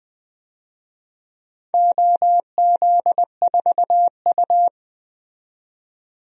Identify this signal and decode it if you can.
Morse code, heard as a keyed tone, about 20 wpm, 700 Hz. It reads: OZ4U
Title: DXCC-100 Call Signs - 4 Characters - 20wpm